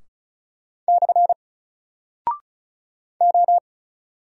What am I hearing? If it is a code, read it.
Morse code, heard as a keyed tone, about 36 wpm, 700 Hz. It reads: / O
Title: Single Character (Letter:Number:Punctuation:Prosign) 35wpm